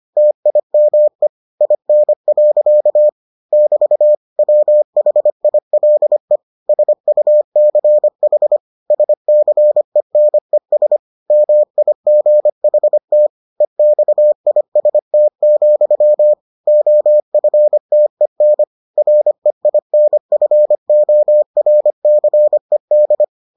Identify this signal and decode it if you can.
Morse code, heard as a keyed tone, about 25 wpm, 600 Hz. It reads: TIME IN. <BT> WHILE SUCH SCENES MIGHT EXIST, OFTEN REINFORCED